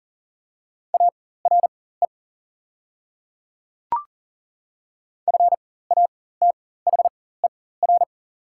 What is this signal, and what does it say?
Morse code, heard as a keyed tone, about 40 wpm, spaced as for 19 wpm, 690 Hz. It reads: ARE FATHER
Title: 4x Character Spacing - Top 200 Words - 40wpm